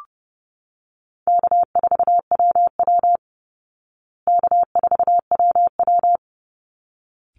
Morse code, heard as a keyed tone, about 30 wpm, 700 Hz. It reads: K4WW K4WW
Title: Easy Call Signs Repeated 2x - 30wpm